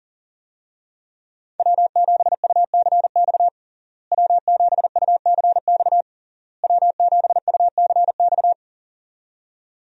Morse code, heard as a keyed tone, about 40 wpm, 700 Hz. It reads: W7UCX W7UCX W7UCX
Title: USA Call Signs - 5 Characters - Repeated 3x - 40wpm